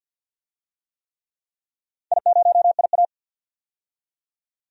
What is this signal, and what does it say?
Morse code, heard as a keyed tone, about 52 wpm, 700 Hz. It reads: I0IA
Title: DXCC-100 Call Signs - 4 Characters - 50wpm